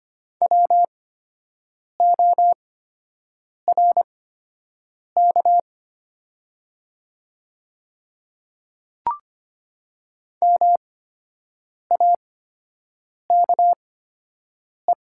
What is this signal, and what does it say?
Morse code, heard as a keyed tone, about 25 wpm, spaced as for 7 wpm, 690 Hz. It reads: WORK MAKE
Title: 8x Character Spacing - Top 300 Words - 25wpm